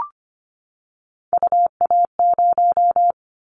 Morse code, heard as a keyed tone, about 25 wpm, 700 Hz. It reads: UA0
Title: Callsign Prefixes for Top 50 DXCC - 25wpm